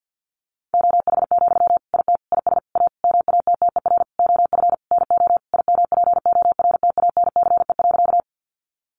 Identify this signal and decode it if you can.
Morse code, heard as a keyed tone, about 50 wpm, 700 Hz. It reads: OH, IT IS A MATTER OF NO IMPORTANCE.
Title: Sentences from Top 1200 Words 50wpm